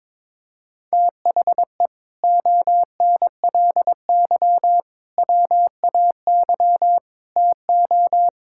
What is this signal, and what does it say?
Morse code, heard as a keyed tone, about 22 wpm, 700 Hz. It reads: THE ONLY WAY TO
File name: practice / 22wpm / Sets of 8 Words - 22wpm